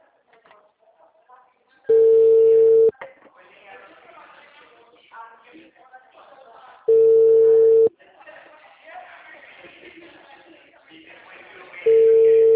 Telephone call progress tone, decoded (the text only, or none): ringback tone